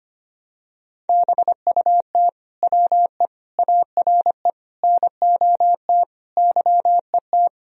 Morse code, heard as a keyed tone, about 25 wpm, 700 Hz. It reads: BUT WE ARE NOT YET
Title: Sentences from Top 1000 Words - Review Part 1 - 25wpm